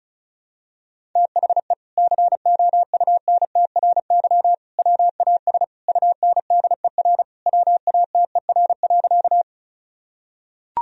Morse code, heard as a keyed tone, about 36 wpm, 700 Hz. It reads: THE COUNTRY WAS UNDER WATER.
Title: Sentences from Top 1000 Words - Review Part 1 - 35wpm